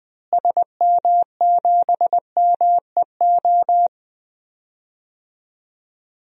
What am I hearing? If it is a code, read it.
Morse code, heard as a keyed tone, about 20 wpm, 700 Hz. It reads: SM7MEO
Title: DXCC-100 Call Signs - 6 Characters - Repeated 3x - 20wpm